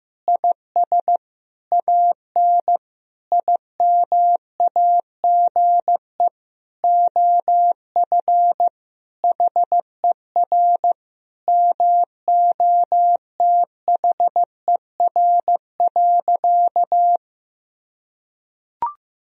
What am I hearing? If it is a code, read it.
Morse code, heard as a keyed tone, about 15 wpm, 700 Hz. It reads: IS AN IMAGE OF HER MOTHER.
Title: Sentences from Top 1000 Words - Review Part 4 - 15wpm